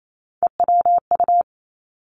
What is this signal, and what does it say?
Morse code, heard as a keyed tone, about 28 wpm, 700 Hz. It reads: EWU
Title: Three Challenging Letter-Number Combinations 28wpm